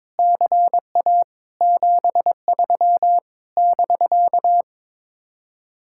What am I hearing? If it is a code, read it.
Morse code, heard as a keyed tone, about 22 wpm, 700 Hz. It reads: CA 73 <BK>
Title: Speed-Racing Common QSO Phrases - 22wpm